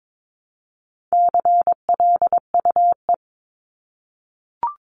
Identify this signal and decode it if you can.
Morse code, heard as a keyed tone, about 22 wpm, 700 Hz. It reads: CLUE
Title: Top 1751-2000 Words 22wpm